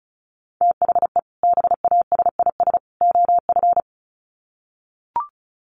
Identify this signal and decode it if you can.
Morse code, heard as a keyed tone, about 34 wpm, 700 Hz. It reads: THE BASIS OF
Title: Sets of 3 Words 35wpm